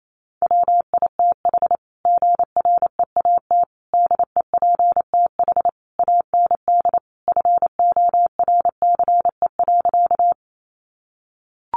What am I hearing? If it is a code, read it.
Morse code, heard as a keyed tone, about 28 wpm, 700 Hz. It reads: WITH GREAT DEPTH AND FORCE.
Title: Sentences from Top 1200 Words 28wpm